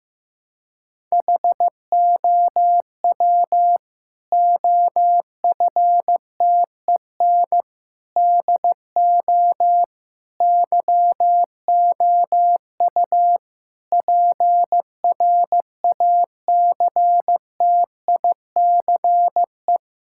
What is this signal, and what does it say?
Morse code, heard as a keyed tone, about 15 wpm, 700 Hz. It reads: HOW OFTEN DO YOU PRACTICE
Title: Sentences from Top 1000 Words - Review Part 6 - 15wpm